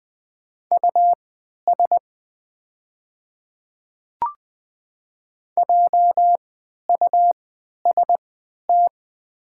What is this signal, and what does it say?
Morse code, heard as a keyed tone, about 20 wpm, spaced as for 11 wpm, 700 Hz. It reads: US JUST
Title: 3x Character Spacing - Top 200 Words - 20wpm